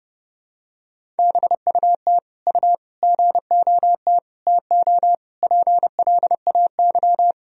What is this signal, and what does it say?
Morse code, heard as a keyed tone, about 30 wpm, 700 Hz. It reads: BUT U GOT TO PLAY
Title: Sentences from Top 300 Words and Abbreviations 30wpm